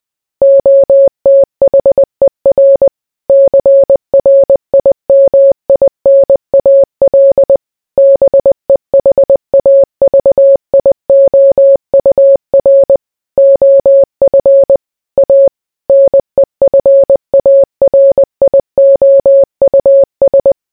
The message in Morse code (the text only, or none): OTHER CRIMINAL BEHAVIOUR OF A NEFARIOUS